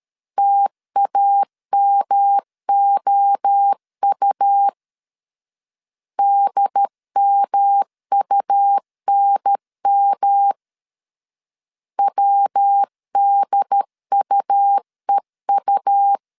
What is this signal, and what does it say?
Morse code, heard as a keyed tone, about 13 wpm, 800 Hz. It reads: TAMOU DMUNM WDUEU
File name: CW / DET / 1 3 1500C S A 12wpm 2905